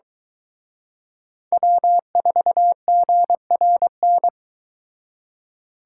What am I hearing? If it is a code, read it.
Morse code, heard as a keyed tone, about 23 wpm, 700 Hz. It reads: W4GRN